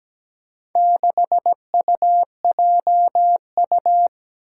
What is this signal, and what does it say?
Morse code, heard as a keyed tone, about 17 wpm, 700 Hz. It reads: 6UJU
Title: Four Challenging Letter-Number Combinations 17wpm